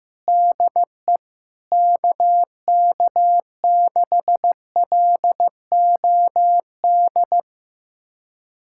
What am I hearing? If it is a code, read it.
Morse code, heard as a keyed tone, about 16 wpm, 700 Hz. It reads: DE KK6LOD
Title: USA Callsign DE Callsign - 6 Characters - 15wpm